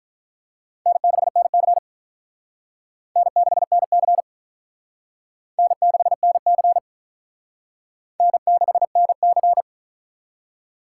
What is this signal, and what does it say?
Morse code, heard as a keyed tone, about 44 wpm, 700 Hz. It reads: N6NC N6NC N6NC N6NC
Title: Speed Racing - USA 1x2 Call Signs 35wpm